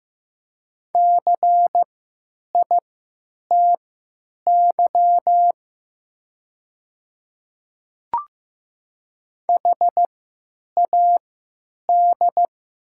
Morse code, heard as a keyed tone, about 15 wpm, spaced as for 9 wpm, 690 Hz. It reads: CITY HAD
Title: 3x Character Spacing - Top 300 Words - 15wpm